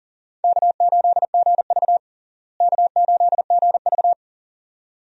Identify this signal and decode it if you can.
Morse code, heard as a keyed tone, about 37 wpm, 700 Hz. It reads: K8GV K8GV
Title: USA Call Signs - 4 Characters - Repeated 2x - 40wpm